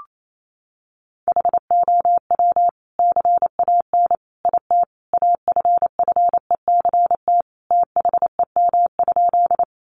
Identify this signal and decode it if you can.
Morse code, heard as a keyed tone, about 28 wpm, 700 Hz. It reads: HOW CAN IT AFFECT THEM?
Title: Sentences from Top 900 Words 28wpm